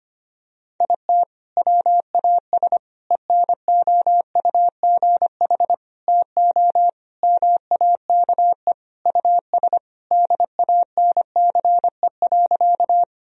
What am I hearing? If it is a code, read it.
Morse code, heard as a keyed tone, about 25 wpm, 700 Hz. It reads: IT WAS ENOUGH TO MAKE US DANCE.
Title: Sentences from Top 1000 Words - Review Part 4 - 25wpm